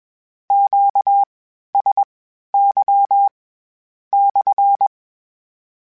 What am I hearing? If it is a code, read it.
Morse code, heard as a keyed tone, about 21 wpm, spaced as for 12 wpm, 800 Hz. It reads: QSY /
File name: Q Calls / Q Calls-mp3s / Q CALLS SPSL @20